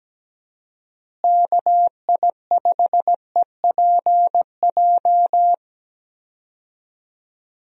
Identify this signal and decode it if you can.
Morse code, heard as a keyed tone, about 17 wpm, 700 Hz. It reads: KI5EPJ
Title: USA Call Signs - 6 Characters - Repeated 2x - 17wpm